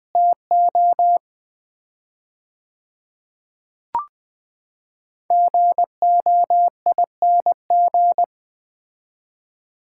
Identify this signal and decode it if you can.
Morse code, heard as a keyed tone, about 20 wpm, 700 Hz. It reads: TO GOING
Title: 4x Word-Spacing - Sets of 4 Words - 20wpm